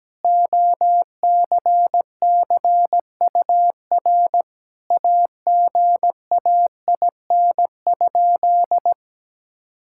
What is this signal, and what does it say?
Morse code, heard as a keyed tone, about 17 wpm, 700 Hz. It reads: OCCUR AGAIN?
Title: Sentences from Top 1000 Words - Review Part 6 - 17wpm